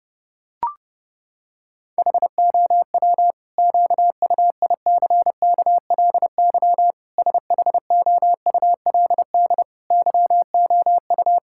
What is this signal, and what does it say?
Morse code, heard as a keyed tone, about 30 wpm, 700 Hz. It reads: HOW QUICKLY SHOULD YOU
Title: Sentences from Top 1000 Words - Review Part 5 - 30wpm